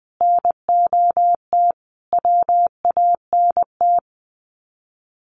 Morse code, heard as a keyed tone, about 20 wpm, 700 Hz. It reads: NOT WANT
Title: Sets of 5 Words Encore2 20wpm